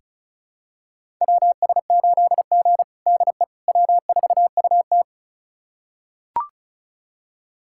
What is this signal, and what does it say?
Morse code, heard as a keyed tone, about 36 wpm, 700 Hz. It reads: WS8G DE W4UT
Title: USA Callsign DE Callsign - 4 Characters - 35wpm